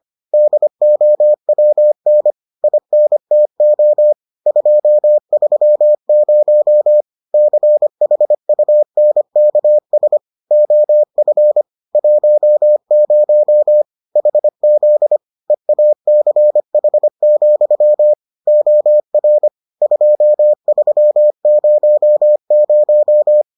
Morse code, heard as a keyed tone, about 25 wpm, 600 Hz. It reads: DOWN INTO 230 CHUNKS OF 10 HZ EACH, OR 2300